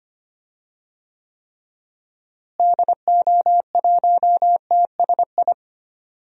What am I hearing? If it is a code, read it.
Morse code, heard as a keyed tone, about 25 wpm, 700 Hz. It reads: DO1TSI
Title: DXCC-100 Call Signs - 6 Characters - Repeated 3x - 25wpm